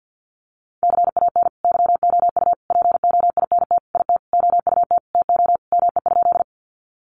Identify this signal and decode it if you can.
Morse code, heard as a keyed tone, about 52 wpm, 700 Hz. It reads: CAN YOU POINT IT OUT TO ME?
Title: Sentences from Top 200 Words 50wpm